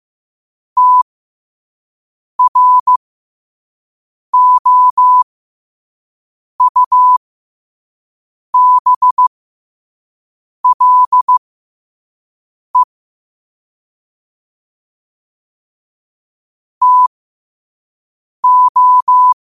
Morse code, heard as a keyed tone, about 15 wpm, spaced as for 5 wpm, 1000 Hz. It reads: TROUBLE TO
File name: qotdfiles / 5wpm